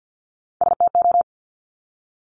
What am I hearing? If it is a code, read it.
Morse code, heard as a keyed tone, about 50 wpm, 700 Hz. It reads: STO
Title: Top Three Letter Combinations in Words 50wpm